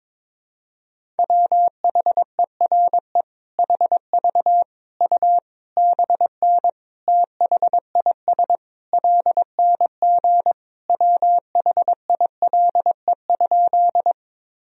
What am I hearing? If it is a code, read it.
Morse code, heard as a keyed tone, about 22 wpm, 700 Hz. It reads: WHERE HV U BN THIS LNG WHILE?
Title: Sentences from Top 300 Words and Abbreviations 22wpm